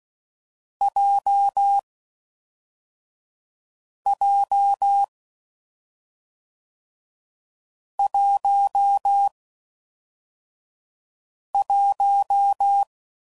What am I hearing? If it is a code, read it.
Morse code, heard as a keyed tone, about 16 wpm, 800 Hz. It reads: J J 1 1
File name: Combinations / Difficult Combos / awj1 ndb6 gz7